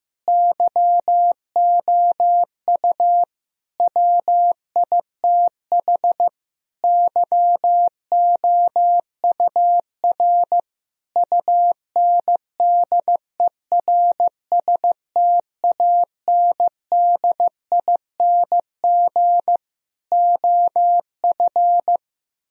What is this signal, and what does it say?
Morse code, heard as a keyed tone, about 15 wpm, 700 Hz. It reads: YOU WITH YOUR UNDERSTANDING OF
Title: Sentences from Top 1000 Words - Review Part 3 - 15wpm